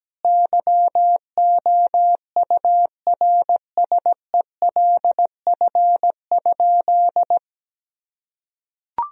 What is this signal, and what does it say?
Morse code, heard as a keyed tone, about 17 wpm, 700 Hz. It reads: YOURSELF?
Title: Sentences from Top 1000 Words - Review Part 4 - 17wpm